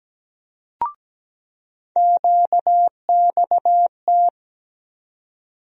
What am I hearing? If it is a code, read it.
Morse code, heard as a keyed tone, about 17 wpm, 700 Hz. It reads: QXT
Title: USA Call Sign Postfixes - 3 Letters 17wpm